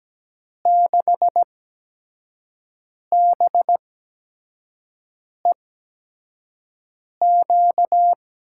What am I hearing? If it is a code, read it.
Morse code, heard as a keyed tone, about 17 wpm, spaced as for 5 wpm, 700 Hz. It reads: 6BEQ
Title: USA Call Signs - 6 Characters - 8x Character Spacing - 17wpm